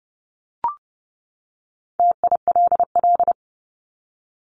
Morse code, heard as a keyed tone, about 30 wpm, 700 Hz. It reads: TILL
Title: Top 1751-2000 Words 30wpm